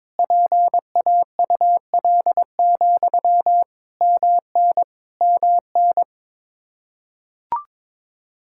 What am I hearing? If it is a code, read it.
Morse code, heard as a keyed tone, about 22 wpm, 700 Hz. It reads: PAUL, MN MN
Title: Practice QSO - Exchange 1 - Beginning - 22wpm